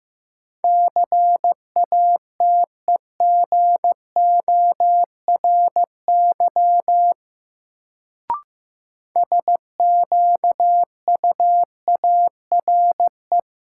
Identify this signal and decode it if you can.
Morse code, heard as a keyed tone, about 15 wpm, 700 Hz. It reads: CATEGORY SQUARE
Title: Sentences from Top 1000 Words - Review Part 7 - New Word Drill - 15wpm